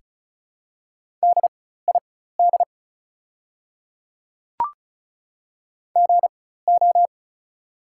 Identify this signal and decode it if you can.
Morse code, heard as a keyed tone, about 34 wpm, spaced as for 16 wpm, 700 Hz. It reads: DID GO
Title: 4x Character Spacing - Top 300 Words - 35wpm